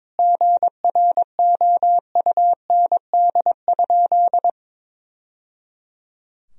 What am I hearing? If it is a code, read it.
Morse code, heard as a keyed tone, about 22 wpm, 700 Hz. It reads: GROUND?
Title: Sentences from Top 1000 Words - Review Part 2 - 22wpm